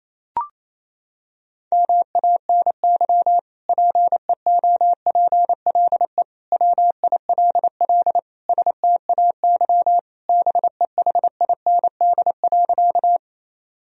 Morse code, heard as a keyed tone, about 28 wpm, 700 Hz. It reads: MANY PEOPLE WILL STAY BEHIND.
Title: Sentences from Top 1000 Words - Review Part 1 - 28wpm